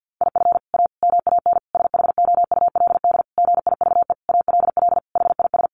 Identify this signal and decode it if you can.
Morse code, heard as a keyed tone, about 50 wpm, 700 Hz. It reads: IF A MAN SHOULD GIVE ALL HIS
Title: Sentences from Top 1500 Words 50wpm